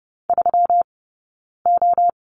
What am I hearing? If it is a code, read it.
Morse code, heard as a keyed tone, about 30 wpm, 700 Hz. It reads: 3 O
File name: Lesson 20d - Number 3 - Two Characters 30wpm